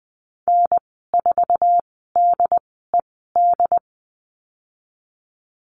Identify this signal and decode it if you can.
Morse code, heard as a keyed tone, about 20 wpm, spaced as for 14 wpm, 700 Hz. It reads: N4DED
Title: 076 Lesson 13g - Letter L - Call Signs 20wpm